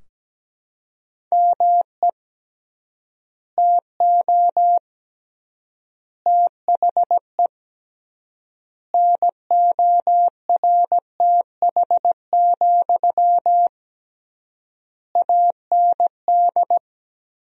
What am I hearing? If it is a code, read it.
Morse code, heard as a keyed tone, about 17 wpm, 700 Hz. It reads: ME TO THE NORTH, AND
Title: 3x Word-Spacing - Sentences from Top 400 Words - 17wpm